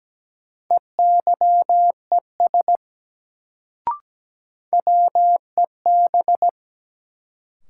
Morse code, heard as a keyed tone, about 17 wpm, 700 Hz. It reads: EYES WEB
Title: Sentences from Top 1000 Words - Review Part 4 - New Word Drill - 17wpm